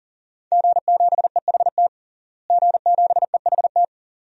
Morse code, heard as a keyed tone, about 37 wpm, 700 Hz. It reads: G7EHT G7EHT
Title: DXCC-100 Call Signs - 5 Characters - Repeated 2x - 40wpm